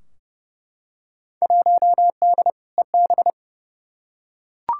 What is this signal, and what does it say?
Morse code, heard as a keyed tone, about 30 wpm, 700 Hz. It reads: 1D EB
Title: Field Day Exchanges - 30wpm